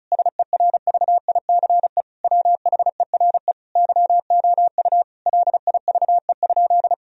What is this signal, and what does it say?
Morse code, heard as a keyed tone, about 36 wpm, 700 Hz. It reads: SERVICE WHERE YOU LIVE?
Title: Sentences from Top 1000 Words - Review Part 4 - 35wpm